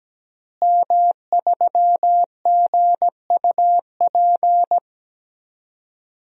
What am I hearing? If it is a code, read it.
Morse code, heard as a keyed tone, about 17 wpm, 700 Hz. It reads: M3GUP
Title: DXCC-100 Call Signs - 5 Characters - Repeated 2x - 17wpm